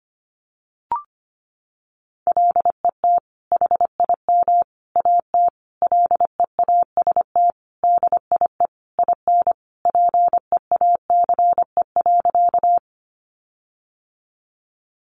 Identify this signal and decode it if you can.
Morse code, heard as a keyed tone, about 25 wpm, 700 Hz. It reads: LET HIM AT LEAST DIE IN PEACE.
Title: Sentences from Top 1500 Words 25wpm